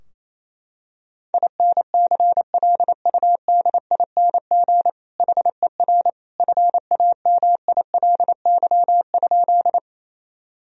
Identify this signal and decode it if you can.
Morse code, heard as a keyed tone, about 28 wpm, 700 Hz. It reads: INCLUDING HER FAMILY?
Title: Sentences from Top 1000 Words - Review Part 4 - 28wpm